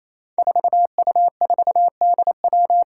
Morse code, heard as a keyed tone, about 28 wpm, 700 Hz. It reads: 4U4DW
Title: Five Challenging Letter-Number Combinations 28wpm